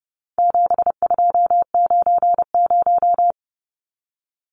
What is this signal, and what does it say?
Morse code, heard as a keyed tone, about 30 wpm, 700 Hz. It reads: 7290
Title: Numbers - 4-Digit 30wpm